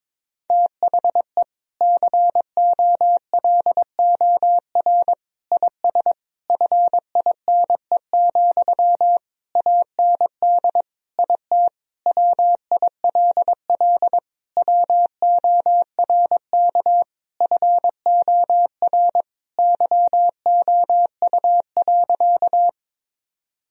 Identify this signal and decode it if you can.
Morse code, heard as a keyed tone, about 22 wpm, 700 Hz. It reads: THE COLOR IS FINE, AND IT WILL WORK FOR YOU.
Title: Sentences from Top 1000 Words - Review Part 2 - 22wpm